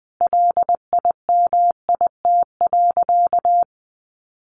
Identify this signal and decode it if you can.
Morse code, heard as a keyed tone, about 20 wpm, 700 Hz. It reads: LIMIT.
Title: Sentences from Top 1500 Words 20wpm